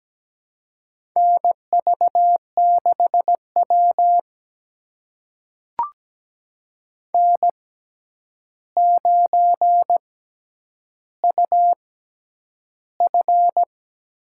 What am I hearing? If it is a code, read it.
Morse code, heard as a keyed tone, about 17 wpm, spaced as for 9 wpm, 700 Hz. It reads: NV6W N9UF
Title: USA Call Signs - 4 Characters - 6x Character Spacing - 17wpm